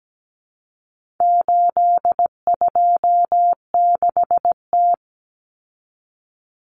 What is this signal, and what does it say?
Morse code, heard as a keyed tone, about 17 wpm, 700 Hz. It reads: 826T
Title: Common Cut Numbers - 4-Digit 17wpm